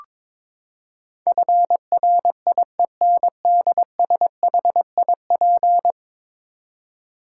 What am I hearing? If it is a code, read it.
Morse code, heard as a keyed tone, about 22 wpm, 700 Hz. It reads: FRIENDSHIP
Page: morse-code-ninja-practice-22wpm